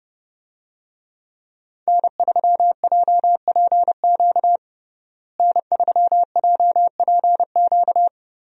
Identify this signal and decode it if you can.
Morse code, heard as a keyed tone, about 32 wpm, 700 Hz. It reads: N3JPQ N3JPQ
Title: USA Call Signs - 5 Characters - Repeated 2x - 30wpm